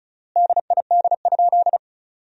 Morse code, heard as a keyed tone, about 34 wpm, 700 Hz. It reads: DID?
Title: 3x Word-Spacing - Sentences from Top 100 Words - 35wpm